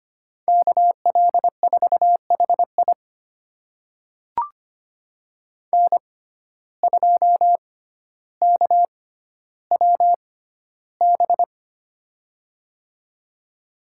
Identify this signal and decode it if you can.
Morse code, heard as a keyed tone, about 25 wpm, spaced as for 13 wpm, 700 Hz. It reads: KL4HI N2KWB
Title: USA Call Signs - 6x Character Spacing - 25wpm